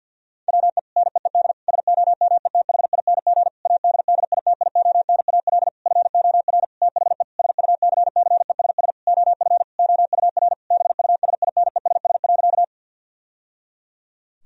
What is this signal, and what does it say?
Morse code, heard as a keyed tone, about 50 wpm, 700 Hz. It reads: WE NEED SOMETHING ADDITIONAL FOR THE SUCCESS OF OUR BUSINESS.
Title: Sentences from Top 1000 Words - Review Part 4 - 50wpm